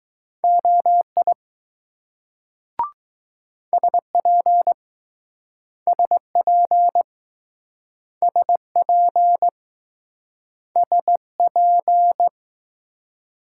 Morse code, heard as a keyed tone, about 20 wpm, 700 Hz. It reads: OI SP SP SP SP